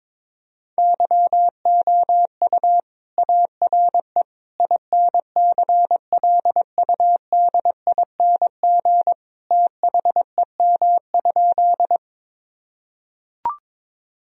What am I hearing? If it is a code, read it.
Morse code, heard as a keyed tone, about 22 wpm, 700 Hz. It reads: YOU ARE INCLUDING THEM?
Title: Sentences from Top 1000 Words - Review Part 4 - 22wpm